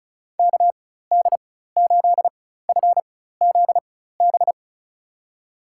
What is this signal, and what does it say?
Morse code, heard as a keyed tone, about 34 wpm, spaced as for 16 wpm, 700 Hz. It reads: KD8FZB
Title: USA Call Signs - 6 Characters - 4x Character Spacing - 35wpm